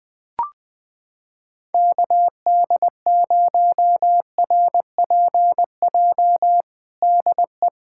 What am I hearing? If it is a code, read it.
Morse code, heard as a keyed tone, about 20 wpm, 700 Hz. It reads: KD0RPJ DE
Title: USA Callsign DE Callsign - 6 Characters - 20wpm